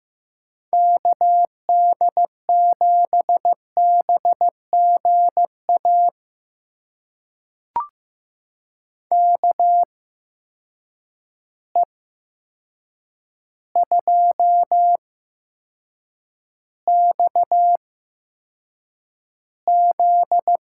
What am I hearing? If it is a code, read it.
Morse code, heard as a keyed tone, about 15 wpm, spaced as for 7 wpm, 700 Hz. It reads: KD7BGA KE2XZ
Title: USA Call Signs - 8x Character Spacing - 15wpm